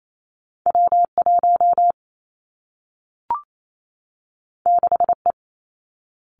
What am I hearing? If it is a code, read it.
Morse code, heard as a keyed tone, about 28 wpm, 700 Hz. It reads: W 1 6 E
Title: Two Challenging Letter-Number Combinations 28wpm